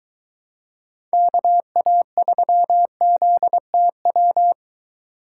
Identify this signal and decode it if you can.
Morse code, heard as a keyed tone, about 22 wpm, 700 Hz. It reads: KA3ZTW